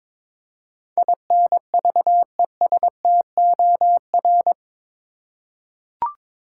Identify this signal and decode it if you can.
Morse code, heard as a keyed tone, about 22 wpm, 700 Hz. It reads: INVESTOR
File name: practice / 22wpm / Eight Letter Words - 22wpm